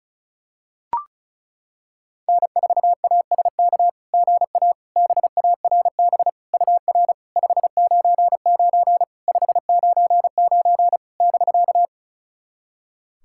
Morse code, heard as a keyed tone, about 36 wpm, 690 Hz. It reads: N4ASK GA BARB UR 599 599 <BK>
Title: SOTA QSO Practice 35wpm